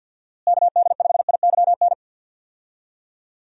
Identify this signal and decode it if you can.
Morse code, heard as a keyed tone, about 50 wpm, 700 Hz. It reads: KD5IYN
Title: USA Call Signs - 6 Characters - Repeated 3x - 50wpm